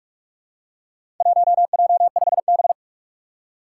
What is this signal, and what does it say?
Morse code, heard as a keyed tone, about 46 wpm, 700 Hz. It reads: 1J5B
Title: Four Challenging Letter-Number Combinations 45wpm